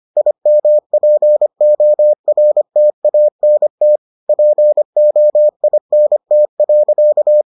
Morse code, heard as a keyed tone, about 25 wpm, 600 Hz. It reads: IMPORTANT POINT.